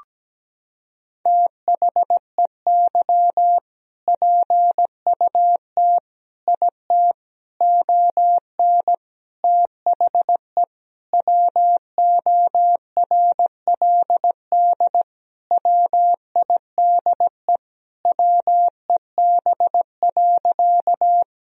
Morse code, heard as a keyed tone, about 17 wpm, 700 Hz. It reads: THEY PUT IT ON THE WORLD WIDE WEB.
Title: Sentences from Top 1000 Words - Review Part 4 - 17wpm